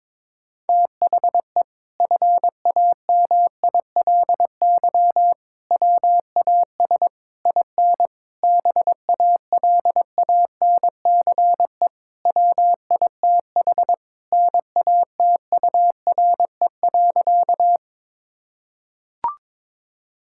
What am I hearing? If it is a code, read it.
Morse code, heard as a keyed tone, about 22 wpm, 700 Hz. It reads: THE FAMILY WAS IN BALANCE WITH NATURE.
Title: Sentences from Top 1000 Words - Review Part 6 - 22wpm